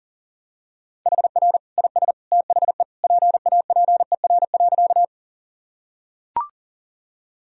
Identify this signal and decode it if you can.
Morse code, heard as a keyed tone, about 40 wpm, 700 Hz. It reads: HR IS THE PAPER.
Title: Sentences from Top 300 Words and Abbreviations 40wpm